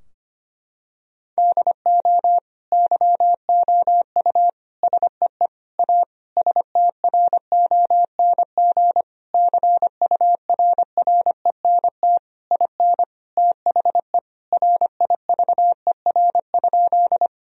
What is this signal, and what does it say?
Morse code, heard as a keyed tone, about 25 wpm, 700 Hz. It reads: DO YOU SEE A STRONG CURRENT IN THE RIVER?
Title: Sentences from Top 1000 Words - Review Part 4 - 25wpm